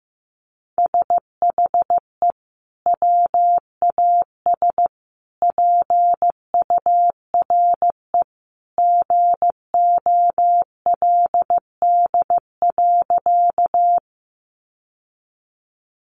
Morse code, heard as a keyed tone, about 15 wpm, 700 Hz. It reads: SHE WAS PURE GOLD.
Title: Sentences from Top 1500 Words 15wpm